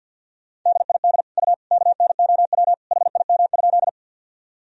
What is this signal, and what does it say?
Morse code, heard as a keyed tone, about 47 wpm, 700 Hz. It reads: DID U KNOW HIM?
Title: Sentences from Top 100 Words and Abbreviations 50wpm